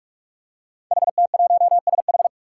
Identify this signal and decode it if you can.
Morse code, heard as a keyed tone, about 44 wpm, 700 Hz. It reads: HT1SH
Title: Five Challenging Letter-Number Combinations 45wpm